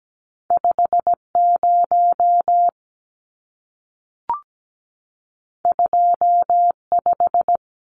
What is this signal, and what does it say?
Morse code, heard as a keyed tone, about 17 wpm, 700 Hz. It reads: 50 25
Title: Numbers - 2-Digit 17wpm